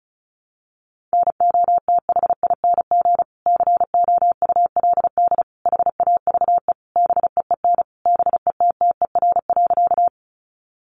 Morse code, heard as a keyed tone, about 34 wpm, 700 Hz. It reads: NOTHING COULD HAVE BEEN BETTER.
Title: Sentences from Top 500 Words 35wpm